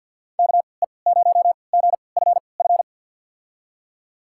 Morse code, heard as a keyed tone, about 53 wpm, spaced as for 29 wpm, 700 Hz. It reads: KE0GFF
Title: USA Call Signs - 6 Characters - 3x Character Spacing - 50wpm